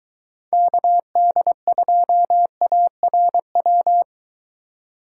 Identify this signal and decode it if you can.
Morse code, heard as a keyed tone, about 23 wpm, 700 Hz. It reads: KD2ARW